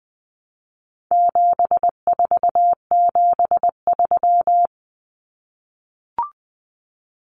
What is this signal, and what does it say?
Morse code, heard as a keyed tone, about 20 wpm, 700 Hz. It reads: 7473
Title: Common Cut Numbers - 4-Digit 20wpm